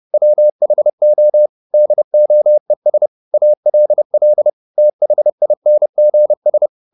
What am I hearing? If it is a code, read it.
Morse code, heard as a keyed tone, about 30 wpm, 600 Hz. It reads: WHO DOES ALL THINGS
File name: the old man and the sea / the old man and the sea 1 30wpm